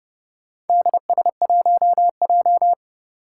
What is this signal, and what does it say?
Morse code, heard as a keyed tone, about 30 wpm, 700 Hz. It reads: DS1J
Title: Four Challenging Letter-Number Combinations 30wpm